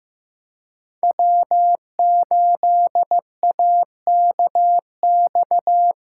Morse code, heard as a keyed tone, about 15 wpm, 700 Hz. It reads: W8AKX
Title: USA Callsign DE Callsign - 5 Characters - 15wpm